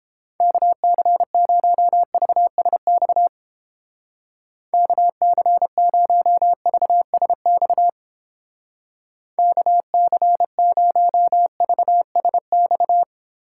Morse code, heard as a keyed tone, about 29 wpm, 700 Hz. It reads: KC0VSX KC0VSX KC0VSX